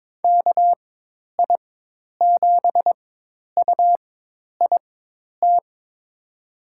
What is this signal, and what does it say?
Morse code, heard as a keyed tone, about 22 wpm, spaced as for 10 wpm, 700 Hz. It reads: KI7UIT
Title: USA Call Signs - 6 Characters - 4x Character Spacing - 22wpm